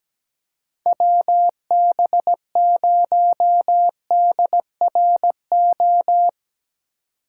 Morse code, heard as a keyed tone, about 17 wpm, 700 Hz. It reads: WB0DRO
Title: USA Call Signs - 6 Characters - 2x Character Spacing - 17wpm